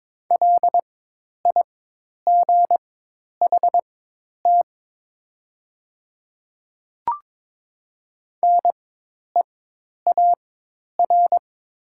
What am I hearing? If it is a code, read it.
Morse code, heard as a keyed tone, about 22 wpm, spaced as for 10 wpm, 700 Hz. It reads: LIGHT NEAR
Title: 4x Character Spacing - Top 300 Words - 22wpm